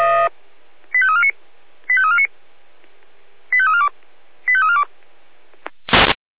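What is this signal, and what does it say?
Five-tone selective call (ZVEI), heard as quick runs of five tones, tones of about 70 ms. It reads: 85349 85349 85342 85342
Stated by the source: Five-tone ZVEI selective call received on an SDR.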